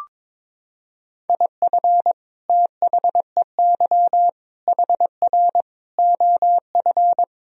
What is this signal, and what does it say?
Morse code, heard as a keyed tone, about 22 wpm, 700 Hz. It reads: IF THEY HR OF
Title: Sentences from Top 400 Words and Abbreviations 22wpm